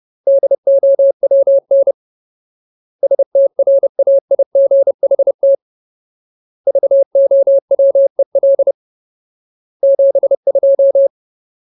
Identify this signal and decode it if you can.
Morse code, heard as a keyed tone, about 30 wpm, 550 Hz. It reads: DOWN STRAIGHT VOWEL 72
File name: Ultimate Word List / 30WPM Ultimate